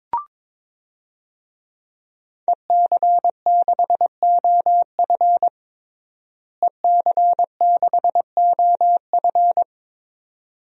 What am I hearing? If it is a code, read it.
Morse code, heard as a keyed tone, about 22 wpm, 700 Hz. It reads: EC6OF EC6OF
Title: DXCC-100 Call Signs - 5 Characters - Repeated 2x - 22wpm